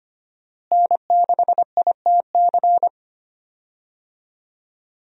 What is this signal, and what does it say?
Morse code, heard as a keyed tone, about 25 wpm, 700 Hz. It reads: N6ITC
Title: USA Call Signs - 5 Characters - Repeated 2x - 25wpm